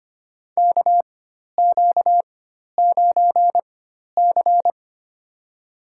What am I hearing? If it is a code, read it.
Morse code, heard as a keyed tone, about 25 wpm, spaced as for 12 wpm, 700 Hz. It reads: KQ9C
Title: USA Call Signs - 4 Characters - 4x Character Spacing - 25wpm